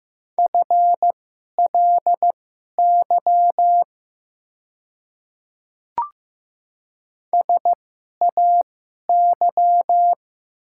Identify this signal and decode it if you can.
Morse code, heard as a keyed tone, about 15 wpm, spaced as for 11 wpm, 700 Hz. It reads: FLY SAY
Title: 2x Character Spacing - Top 500 Words - 15wpm